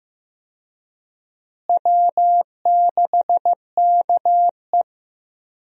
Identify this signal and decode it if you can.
Morse code, heard as a keyed tone, about 15 wpm, 700 Hz. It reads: W6KE
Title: USA Call Signs - 4 Characters - Repeated 3x - 15wpm